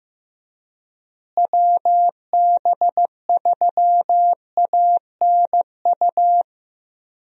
Morse code, heard as a keyed tone, about 15 wpm, 700 Hz. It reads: WB3ANU
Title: USA Call Signs - 6 Characters - Repeated 2x - 15wpm